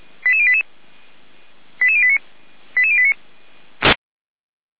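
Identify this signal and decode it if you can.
Five-tone selective call (ZVEI), heard as quick runs of five tones, tones of about 70 ms. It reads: 88088 88089 88089
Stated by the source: ZVEI five-tone selective call received by SDR.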